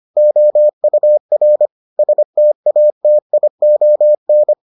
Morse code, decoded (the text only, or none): OUR STATION